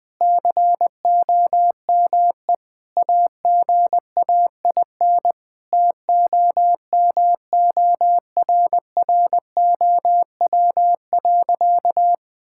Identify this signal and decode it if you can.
Morse code, heard as a keyed tone, about 20 wpm, 700 Hz. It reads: COME AGAIN TOMORROW.
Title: Sentences with 4-Word Phrases - 20wpm